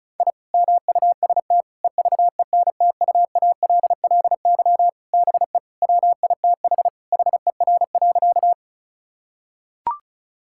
Morse code, heard as a keyed tone, about 36 wpm, 700 Hz. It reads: I MUST EVENTUALLY BE WITH HER.
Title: Sentences from Top 1000 Words - Review Part 6 - 35wpm